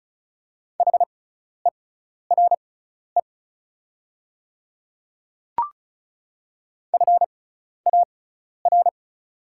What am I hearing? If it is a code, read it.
Morse code, heard as a keyed tone, about 36 wpm, spaced as for 12 wpm, 700 Hz. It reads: HERE FAR
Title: 6x Character Spacing - Top 300 Words - 35wpm